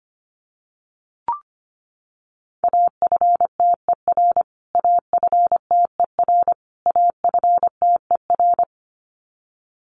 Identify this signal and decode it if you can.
Morse code, heard as a keyed tone, about 25 wpm, 700 Hz. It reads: AFTER AFTER AFTER
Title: Top 200 Words Repeated 25wpm